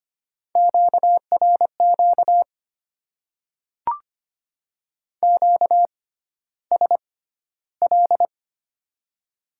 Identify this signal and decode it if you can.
Morse code, heard as a keyed tone, about 25 wpm, spaced as for 13 wpm, 700 Hz. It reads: QRQ QSL
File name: Top 90 QSO Elements - 6x Character Spacing - 25wpm